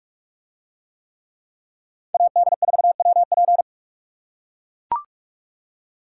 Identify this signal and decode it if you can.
Morse code, heard as a keyed tone, about 44 wpm, 700 Hz. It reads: AD4WP